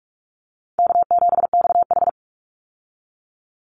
Morse code, heard as a keyed tone, about 44 wpm, 700 Hz. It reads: K7XH
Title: Call Signs - 45wpm